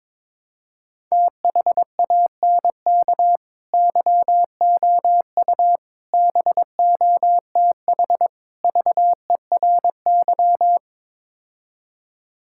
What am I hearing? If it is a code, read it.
Morse code, heard as a keyed tone, about 21 wpm, 700 Hz. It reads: THANK YOU BOTH VERY
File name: practice / 22wpm / Build A Phrase - 5-Word Phrases - 22wpm